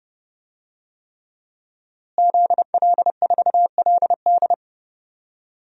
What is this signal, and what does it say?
Morse code, heard as a keyed tone, about 30 wpm, 700 Hz. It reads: ZL4LD
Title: DXCC-100 Call Signs - 5 Characters - Repeated 3x - 30wpm